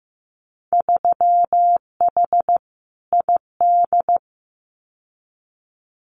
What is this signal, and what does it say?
Morse code, heard as a keyed tone, about 15 wpm, 700 Hz. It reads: 3H ID
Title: Winter Field Day Exchanges Encore 15wpm